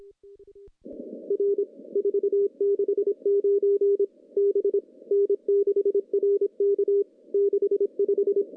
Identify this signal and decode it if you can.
Morse code, heard as a keyed tone, about 26 wpm, 400 Hz. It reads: R 469 B N6RK 65
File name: contests / CW-SS-k5tr- -0400